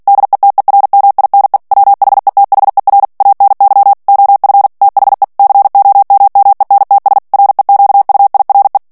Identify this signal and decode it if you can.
Morse code, heard as a keyed tone, about 48 wpm, 800 Hz. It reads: DETERMINE WHETHER ANY OF THE COMMENTS REQUIRE
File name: ANGLAIS texte clair 3 vitesse 45 mots minute